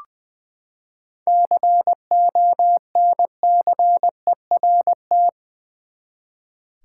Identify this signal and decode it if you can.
Morse code, heard as a keyed tone, about 20 wpm, 700 Hz. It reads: CONCERT
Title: Seven Letter Words - 20wpm